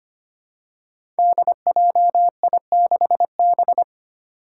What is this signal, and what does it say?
Morse code, heard as a keyed tone, about 25 wpm, 700 Hz. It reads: DJI6B
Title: Five Challenging Letter-Number Combinations 25wpm